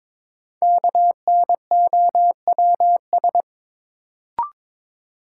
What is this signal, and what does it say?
Morse code, heard as a keyed tone, about 22 wpm, 700 Hz. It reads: KNOWS
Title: Sentences from Top 1000 Words - Review Part 1 - New Word Drill - 22wpm